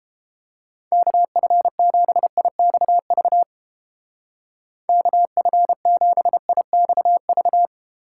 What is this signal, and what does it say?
Morse code, heard as a keyed tone, about 30 wpm, 700 Hz. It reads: KF7IXV KF7IXV